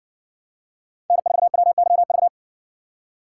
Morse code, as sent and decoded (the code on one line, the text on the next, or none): -. ....- .-- -.-- ...-
N4WYV